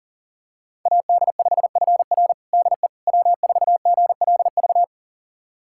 Morse code, heard as a keyed tone, about 40 wpm, 700 Hz. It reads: AD5FR DE W4GLV
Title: USA Callsign DE Callsign - 5 Characters - 40wpm